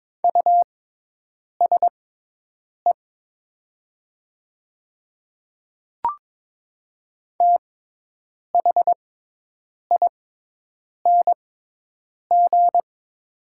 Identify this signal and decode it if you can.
Morse code, heard as a keyed tone, about 22 wpm, spaced as for 8 wpm, 700 Hz. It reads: USE THING
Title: 6x Character Spacing - Top 200 Words - 22wpm